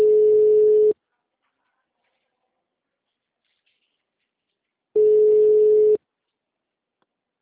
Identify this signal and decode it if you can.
Ringback tone (Europe, 425 Hz).
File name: out- -2006- - - .80